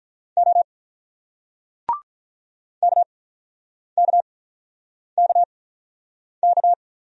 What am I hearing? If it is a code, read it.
Morse code, heard as a keyed tone, about 45 wpm, spaced as for 9 wpm, 690 Hz. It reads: O KKKK